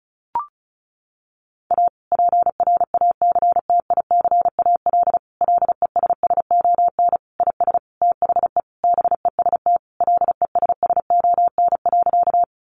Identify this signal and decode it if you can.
Morse code, heard as a keyed tone, about 34 wpm, 700 Hz. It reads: A PRACTICAL LESSON IS THE BEST LESSON.
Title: Sentences from Top 1400 Words 35wpm